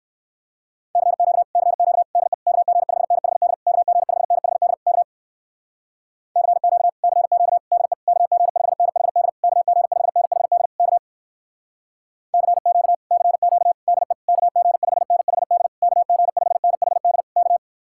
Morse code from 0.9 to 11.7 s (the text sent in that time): CQ CQ DE KG5NHD KG5NHD K CQ CQ DE KG5NHD KG5NHD K